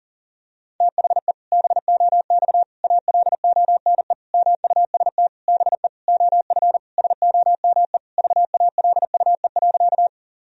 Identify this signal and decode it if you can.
Morse code, heard as a keyed tone, about 37 wpm, 700 Hz. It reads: THE BOX ALONE MUST BE OF SOME VALUE.
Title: Sentences from Top 1000 Words - Review Part 3 - 40wpm